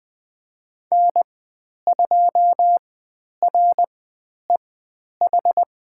Morse code, heard as a keyed tone, about 20 wpm, spaced as for 10 wpm, 700 Hz. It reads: N2REH
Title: 102 Lesson 17g - Letter u - Call Signs 20wpm